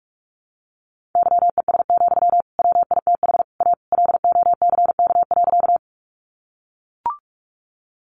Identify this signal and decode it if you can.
Morse code, heard as a keyed tone, about 44 wpm, 700 Hz. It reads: YES, WITH A LOCK.
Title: Sentences from Top 1300 Words 45wpm